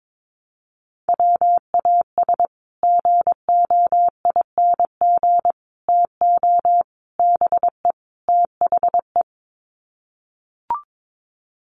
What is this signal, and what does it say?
Morse code, heard as a keyed tone, about 22 wpm, 700 Hz. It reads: WAS GOING TO BE THE
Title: Sets of 5 Words Encore2 22wpm